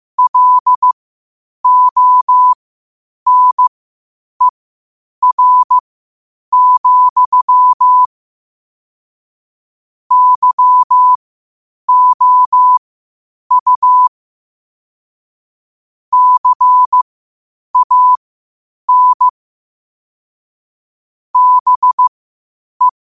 Morse code, heard as a keyed tone, about 15 wpm, spaced as for 8 wpm, 1000 Hz. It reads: LONER, YOU CAN BE
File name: qotdfiles / 8wpm